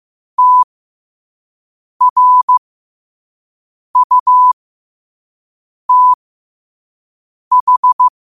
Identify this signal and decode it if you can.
Morse code, heard as a keyed tone, about 15 wpm, spaced as for 5 wpm, 1000 Hz. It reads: TRUTH